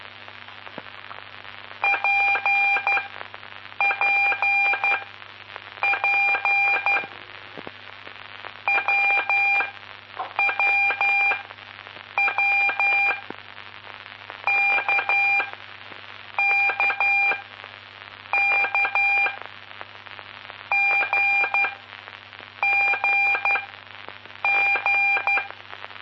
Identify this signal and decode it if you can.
Morse code, heard as a keyed tone, about 12 wpm, spaced as for 7 wpm, 2400 Hz, with noise. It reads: PPP WWW KKK GGG
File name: pp5bar / aulas de telegrafia / Aula-11